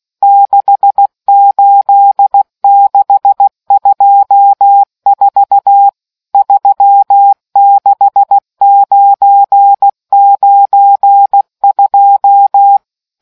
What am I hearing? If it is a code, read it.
Morse code, heard as a keyed tone, about 16 wpm, 800 Hz. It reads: 68624 36992